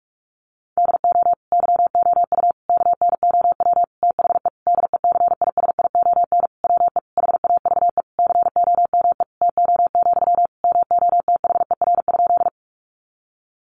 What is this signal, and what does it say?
Morse code, heard as a keyed tone, about 44 wpm, 700 Hz. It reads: DO YOU KNOW THE DECISION WE HAVE COME TO, MOTHER?
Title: Sentences from Top 1000 Words 45wpm